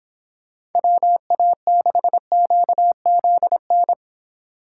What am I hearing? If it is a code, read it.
Morse code, heard as a keyed tone, about 26 wpm, 700 Hz. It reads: WA6QZN